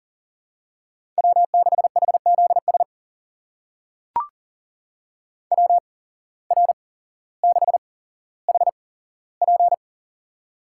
Morse code, heard as a keyed tone, about 40 wpm, spaced as for 17 wpm, 700 Hz. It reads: W6HZS WR6HP
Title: USA Call Signs - 8x Character Spacing - 40wpm